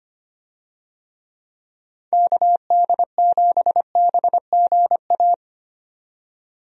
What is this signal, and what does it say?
Morse code, heard as a keyed tone, about 25 wpm, 700 Hz. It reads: KD7BGA
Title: USA Call Signs - 6 Characters - 25wpm